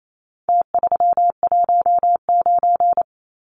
Morse code, heard as a keyed tone, about 28 wpm, 700 Hz. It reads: T319
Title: Numbers - Contest Style - 4-Digit 28wpm